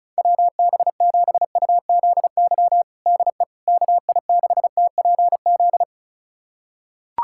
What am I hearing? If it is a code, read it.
Morse code, heard as a keyed tone, about 36 wpm, 700 Hz. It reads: WB7UZY DE KI6TPZ
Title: USA Callsign DE Callsign - 6 Characters - 35wpm